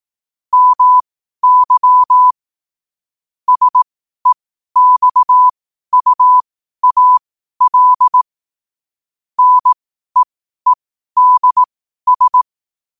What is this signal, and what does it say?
Morse code, heard as a keyed tone, about 18 wpm, spaced as for 12 wpm, 1000 Hz. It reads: MY SEXUAL NEEDS